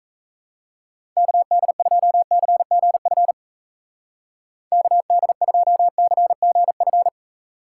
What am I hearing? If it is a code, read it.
Morse code, heard as a keyed tone, about 37 wpm, 700 Hz. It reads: KD2CGF KD2CGF